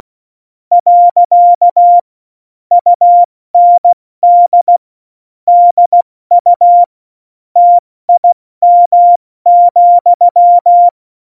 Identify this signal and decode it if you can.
Morse code, heard as a keyed tone, about 16 wpm, 700 Hz. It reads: . UND DU TIM,
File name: Morsekurs / Witze / Witz 30. April 2019, 70 F80